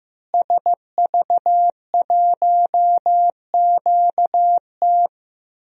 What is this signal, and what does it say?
Morse code, heard as a keyed tone, about 15 wpm, 700 Hz. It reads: SV1QT
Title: DXCC-100 Call Signs - 5 Characters - Repeated 3x - 15wpm